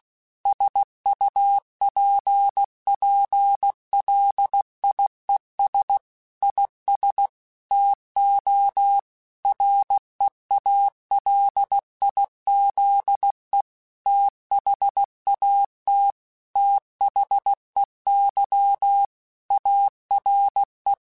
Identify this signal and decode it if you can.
Morse code, heard as a keyed tone, about 16 wpm, 800 Hz. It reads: SUPPLIES IS TO REALIZE THAT THEY ARE
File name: Technical Subjects / Technical Subjects-mp3s / Sw Pwr Sup @15